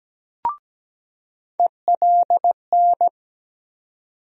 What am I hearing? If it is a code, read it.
Morse code, heard as a keyed tone, about 17 wpm, 700 Hz. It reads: ELN
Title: USA Call Sign Postfixes - 3 Letters 17wpm